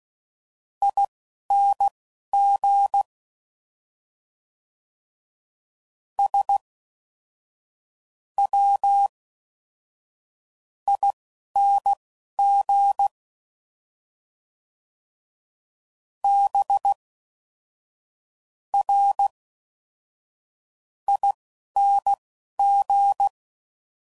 Morse code, heard as a keyed tone, about 16 wpm, spaced as for 11 wpm, 800 Hz. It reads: ING S W ING B R ING